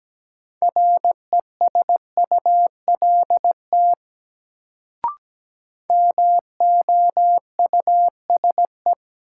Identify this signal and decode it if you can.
Morse code, heard as a keyed tone, about 17 wpm, 700 Hz. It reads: RESULT MOUSE
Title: Sentences from Top 700 Words - New Word Drill - 17wpm